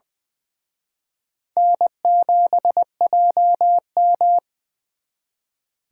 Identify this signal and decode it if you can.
Morse code, heard as a keyed tone, about 25 wpm, 700 Hz. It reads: N7JM
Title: Speed Racing - USA 1x2 Call Signs 20wpm